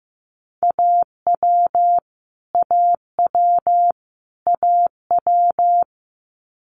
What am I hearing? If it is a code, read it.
Morse code, heard as a keyed tone, about 15 wpm, 700 Hz. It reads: AW AW AW
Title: Multi-letter Phonemes Repeated 3x 15wpm